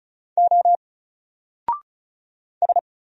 Morse code, heard as a keyed tone, about 34 wpm, 700 Hz. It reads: O S
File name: Single-Letter-Number-35wpm